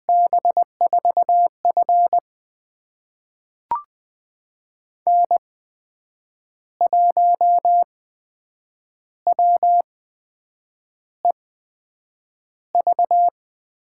Morse code, heard as a keyed tone, about 20 wpm, spaced as for 7 wpm, 700 Hz. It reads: B4F N1WEV
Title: USA Call Signs - 8x Character Spacing - 20wpm